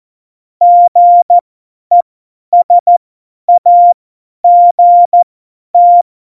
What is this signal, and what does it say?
Morse code, heard as a keyed tone, about 14 wpm, spaced as for 10 wpm, 700 Hz. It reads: GESAGT
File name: Morsekurs / Witze / Witz 18. September 2018, 50 F70